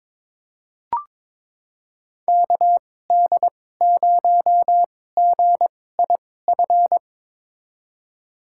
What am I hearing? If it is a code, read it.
Morse code, heard as a keyed tone, about 22 wpm, spaced as for 16 wpm, 700 Hz. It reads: KD0GIF
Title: USA Call Signs - 6 Characters - 2x Character Spacing - 22wpm